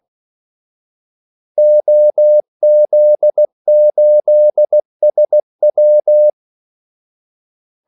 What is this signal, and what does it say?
Morse code, heard as a keyed tone, about 16 wpm, 600 Hz. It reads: OZ8SW
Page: Kaldesignaler 16 wpm